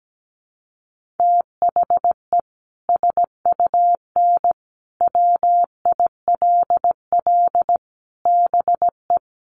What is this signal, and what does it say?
Morse code, heard as a keyed tone, about 17 wpm, 700 Hz. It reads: THE SUN WILL BE
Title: Sentences from Top 900 Words 17wpm